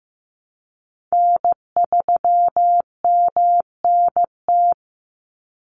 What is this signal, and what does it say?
Morse code, heard as a keyed tone, about 15 wpm, 700 Hz. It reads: N3MNT
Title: Call Signs Repeated 2x - 15wpm